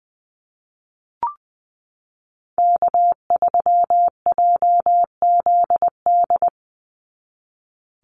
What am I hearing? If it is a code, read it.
Morse code, heard as a keyed tone, about 20 wpm, 700 Hz. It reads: K3JZD
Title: NA SOTA Chaser Callsigns 20wpm